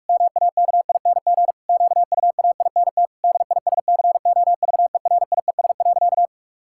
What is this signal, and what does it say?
Morse code, heard as a keyed tone, about 46 wpm, 700 Hz. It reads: MAKING QUAINT DISCOVERIES.
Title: Pangram Challenge v2 - 45wpm